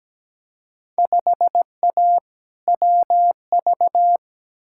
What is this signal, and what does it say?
Morse code, heard as a keyed tone, about 17 wpm, 700 Hz. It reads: 5A WV
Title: Field Day Exchanges - 17wpm